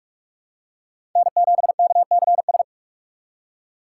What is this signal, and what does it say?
Morse code, heard as a keyed tone, about 46 wpm, 700 Hz. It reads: N7KCS
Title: USA Call Signs - 3x Character Spacing - 45wpm